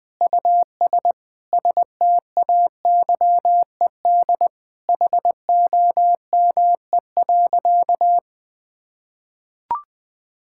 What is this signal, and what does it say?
Morse code, heard as a keyed tone, about 21 wpm, 700 Hz. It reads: US STAYED HOME.
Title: Sentences with 4-Word Phrases - 20wpm